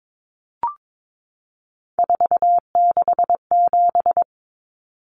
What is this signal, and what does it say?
Morse code, heard as a keyed tone, about 22 wpm, 700 Hz. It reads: 467
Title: Numbers - 3-Digit 22wpm